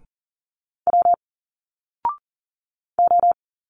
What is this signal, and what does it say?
Morse code, heard as a keyed tone, about 40 wpm, 700 Hz. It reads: W O
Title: Single Letters 40wpm